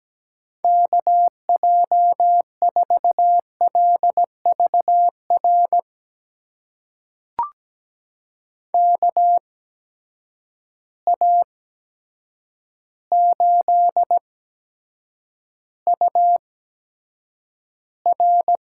Morse code, heard as a keyed tone, about 17 wpm, spaced as for 8 wpm, 700 Hz. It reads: KJ4LVR KA8UR
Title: USA Call Signs - 8x Character Spacing - 17wpm